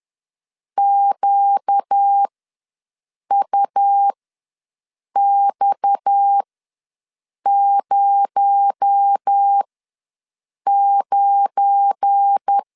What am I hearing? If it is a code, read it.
Morse code, heard as a keyed tone, about 10 wpm, spaced as for 6 wpm, 800 Hz. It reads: QUX09
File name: LEZIONI / 7 GF9Z / 7 GF9 ws 10wpm 4034